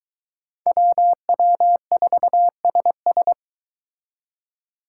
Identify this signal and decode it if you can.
Morse code, heard as a keyed tone, about 23 wpm, 700 Hz. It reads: WW4SS